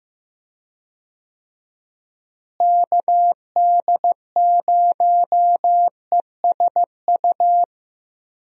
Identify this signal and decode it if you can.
Morse code, heard as a keyed tone, about 15 wpm, 700 Hz. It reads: KD0ESU
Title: USA Call Signs - 6 Characters - Repeated 3x - 15wpm